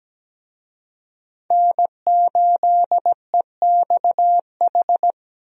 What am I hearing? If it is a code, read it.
Morse code, heard as a keyed tone, about 17 wpm, 700 Hz. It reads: N8EXH
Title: USA Call Signs - 5 Characters - 17wpm